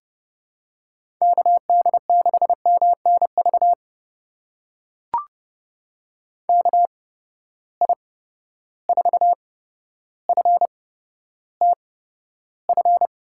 Morse code, heard as a keyed tone, about 30 wpm, spaced as for 13 wpm, 700 Hz. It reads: KD6MNV KI4FTF
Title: USA Call Signs - 6 Characters - 8x Character Spacing - 30wpm